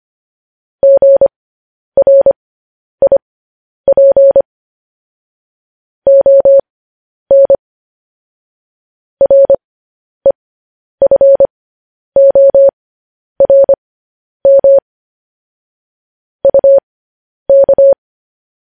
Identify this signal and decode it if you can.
Morse code, heard as a keyed tone, about 25 wpm, spaced as for 10 wpm, 550 Hz. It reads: GRIP ON REFORM UK
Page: news-headlines-in-morse-code-at-10-wpm